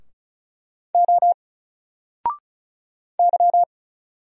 Morse code, heard as a keyed tone, about 35 wpm, 700 Hz. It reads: O Y
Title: Single Letter-Number Alt 35wpm